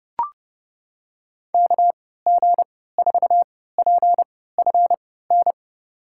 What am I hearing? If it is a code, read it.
Morse code, heard as a keyed tone, about 30 wpm, spaced as for 17 wpm, 700 Hz. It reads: KG4PFN
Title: USA Call Signs - 6 Characters - 3x Character Spacing - 30wpm